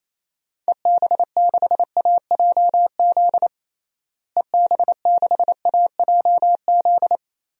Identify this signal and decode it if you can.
Morse code, heard as a keyed tone, about 28 wpm, 700 Hz. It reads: EB6AJZ EB6AJZ
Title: DXCC-100 Call Signs - 6 Characters - Repeated 3x - 28wpm